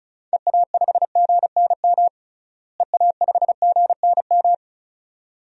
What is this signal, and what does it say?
Morse code, heard as a keyed tone, about 36 wpm, 700 Hz. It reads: EA5GNM EA5GNM
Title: DXCC-100 Call Signs - 6 Characters - Repeated 2x - 35wpm